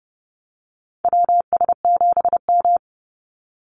Morse code, heard as a keyed tone, about 30 wpm, 700 Hz. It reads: WS7M
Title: Call Signs - 30wpm